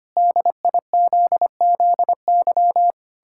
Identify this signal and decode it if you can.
Morse code, heard as a keyed tone, about 25 wpm, 700 Hz. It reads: DIZZY
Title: Pangram Challenge v2 - 25wpm